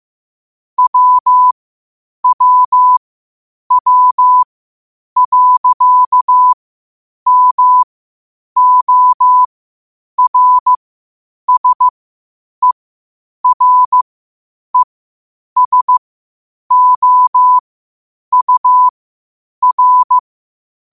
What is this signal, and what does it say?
Morse code, heard as a keyed tone, about 15 wpm, spaced as for 8 wpm, 1000 Hz. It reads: WWW.MORSERESOUR
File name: qotdfiles / 8wpm